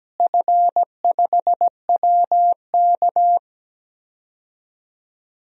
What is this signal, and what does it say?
Morse code, heard as a keyed tone, about 17 wpm, 700 Hz. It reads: F5WK
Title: DXCC-100 Call Signs - 4 Characters - Repeated 2x - 17wpm